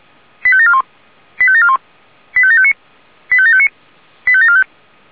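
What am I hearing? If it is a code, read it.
Five-tone selective call (ZVEI), heard as quick runs of five tones, tones of about 70 ms. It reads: 85741 85741 85759 85759 85745
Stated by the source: SDR-received ZVEI five-tone call.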